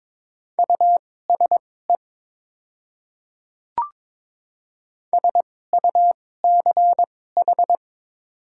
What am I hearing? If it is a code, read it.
Morse code, heard as a keyed tone, about 22 wpm, spaced as for 16 wpm, 700 Hz. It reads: USE SUCH
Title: 2x Character Spacing - Top 300 Words - 22wpm